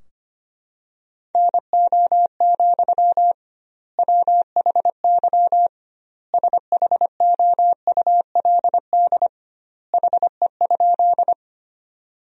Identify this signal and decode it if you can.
Morse code, heard as a keyed tone, about 25 wpm, 700 Hz. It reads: NO, WHY SHOULD HE?
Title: 2x Word-Spacing - Sentences from Top 300 Words - 25wpm